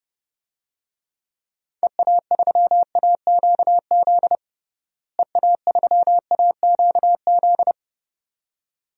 Morse code, heard as a keyed tone, about 30 wpm, 700 Hz. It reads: EA3AQZ EA3AQZ
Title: DXCC-100 Call Signs - 6 Characters - Repeated 2x - 30wpm